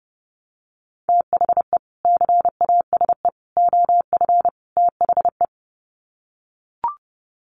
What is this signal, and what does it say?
Morse code, heard as a keyed tone, about 30 wpm, 700 Hz. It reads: THE CASE OF THE
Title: Sets of 4 Words 30wpm